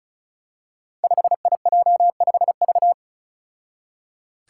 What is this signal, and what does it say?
Morse code, heard as a keyed tone, about 36 wpm, 700 Hz. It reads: 5IJ5V
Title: Five Challenging Letter-Number Combinations 35wpm